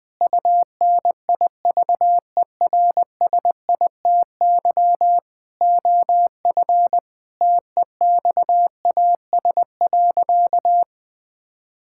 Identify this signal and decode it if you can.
Morse code, heard as a keyed tone, about 20 wpm, 700 Hz. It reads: UNIVERSITY OF TEXAS.
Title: Sentences with 4-Word Phrases - 20wpm